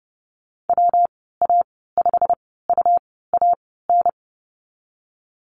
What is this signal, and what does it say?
Morse code, heard as a keyed tone, about 30 wpm, spaced as for 17 wpm, 700 Hz. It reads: WA5UAN
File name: Lesson 19g - Letter w - Call Signs 30wpm